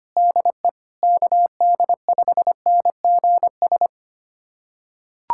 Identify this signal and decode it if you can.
Morse code, heard as a keyed tone, about 25 wpm, 700 Hz. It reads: DE KD5NGS
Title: USA Callsign DE Callsign 25wpm